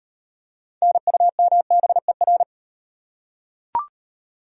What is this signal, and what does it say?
Morse code, heard as a keyed tone, about 39 wpm, 700 Hz. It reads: NUMBER